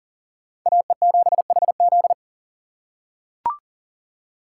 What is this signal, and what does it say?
Morse code, heard as a keyed tone, about 40 wpm, 700 Hz. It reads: AE7HZ
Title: USA Call Signs - 5 Characters - 4x Character Spacing - 40wpm